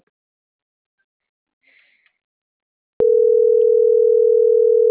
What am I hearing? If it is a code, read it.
Ringback tone (North America), 440 + 480 Hz.